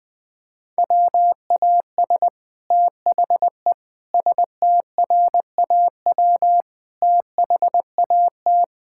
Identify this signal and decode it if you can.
Morse code, heard as a keyed tone, about 20 wpm, 700 Hz. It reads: WAS THE STRAW THAT
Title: Sets of 8 Words - 20wpm